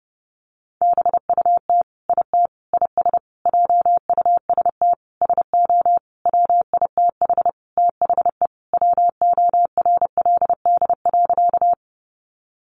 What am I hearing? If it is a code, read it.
Morse code, heard as a keyed tone, about 30 wpm, 700 Hz. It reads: BUT IT IS JUST SO WITH THE WORLD.
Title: Sentences from Top 200 Words 30wpm